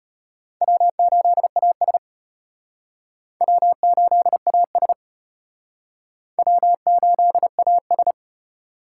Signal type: Morse code

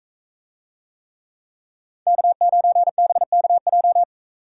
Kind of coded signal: Morse code

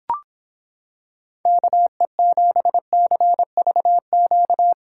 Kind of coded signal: Morse code

